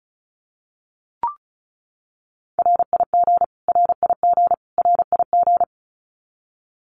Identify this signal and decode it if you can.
Morse code, heard as a keyed tone, about 34 wpm, 700 Hz. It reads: RIG RIG RIG
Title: Abbreviations Repeated 3x - 35wpm